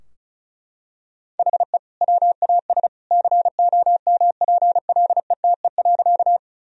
Morse code, heard as a keyed tone, about 34 wpm, 700 Hz. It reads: HE WAS COMPLETE.
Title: Sentences from Top 1000 Words - Review Part 2 - 35wpm